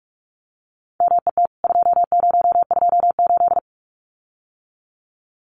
Morse code, heard as a keyed tone, about 44 wpm, 700 Hz. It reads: MET 2028
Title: CWT Exchange 45wpm